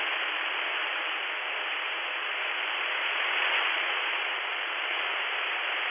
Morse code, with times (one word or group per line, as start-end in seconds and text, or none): none